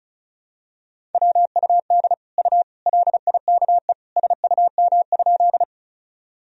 Morse code, heard as a keyed tone, about 34 wpm, 700 Hz. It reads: WUD U LIKE SUM?
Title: Sentences from Top 100 Words and Abbreviations 35wpm